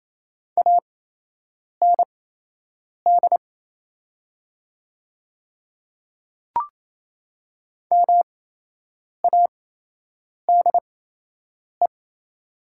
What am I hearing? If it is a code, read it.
Morse code, heard as a keyed tone, about 28 wpm, spaced as for 8 wpm, 700 Hz. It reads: AND MADE
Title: 8x Character Spacing - Top 200 Words - 28wpm